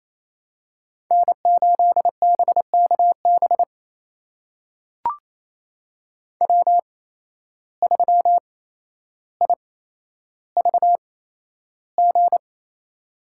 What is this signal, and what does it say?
Morse code, heard as a keyed tone, about 28 wpm, spaced as for 12 wpm, 700 Hz. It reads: N8BKB W3IVG
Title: USA Call Signs - 5 Characters - 8x Character Spacing - 28wpm